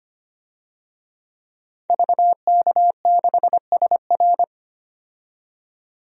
Morse code, heard as a keyed tone, about 25 wpm, 700 Hz. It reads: VK6SR
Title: DXCC-100 Call Signs - 5 Characters - 25wpm